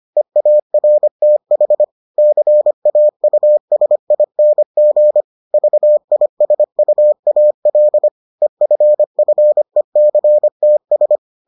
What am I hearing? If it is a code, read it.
Morse code, heard as a keyed tone, about 25 wpm, 600 Hz. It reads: EARTH CAUSING VISUAL EFFECTS